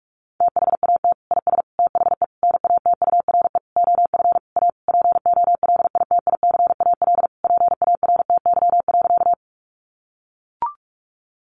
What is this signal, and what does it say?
Morse code, heard as a keyed tone, about 44 wpm, 700 Hz. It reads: THAT IS THE NATURE OF A POLITICAL PARTY.
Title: Sentences from Top 1400 Words 45wpm